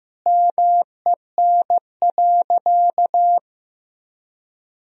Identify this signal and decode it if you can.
Morse code, heard as a keyed tone, about 16 wpm, 700 Hz. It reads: MEN.
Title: Sentences from Top 1000 Words - Review Part 1 - 15wpm